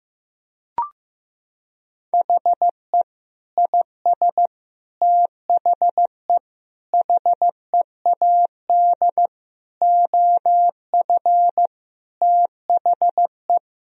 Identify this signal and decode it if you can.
Morse code, heard as a keyed tone, about 15 wpm, 700 Hz. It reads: HE IS THE HEAD OF THE
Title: Sentences from Top 1000 Words - Review Part 4 - 15wpm